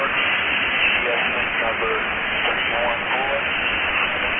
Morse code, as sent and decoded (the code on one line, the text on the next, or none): none